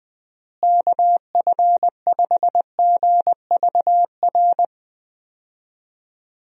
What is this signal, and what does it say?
Morse code, heard as a keyed tone, about 20 wpm, 700 Hz. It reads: KF5GVR
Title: USA Call Signs - 6 Characters - Repeated 3x - 20wpm